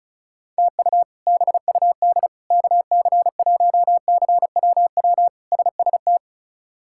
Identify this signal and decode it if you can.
Morse code, heard as a keyed tone, about 36 wpm, 700 Hz. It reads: TU BUD KC1CWW SST
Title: K1USN SST QSO Practice - 35wpm